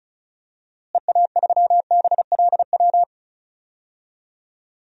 Morse code, heard as a keyed tone, about 34 wpm, 700 Hz. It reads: EA3BLW
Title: DXCC-100 Call Signs - 6 Characters - Repeated 3x - 35wpm